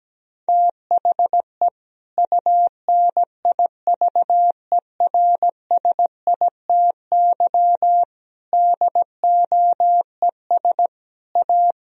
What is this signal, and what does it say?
Morse code, heard as a keyed tone, about 17 wpm, 700 Hz. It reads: THE UNIVERSITY DOES A
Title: Sentences from Top 1000 Words - Review Part 4 - 17wpm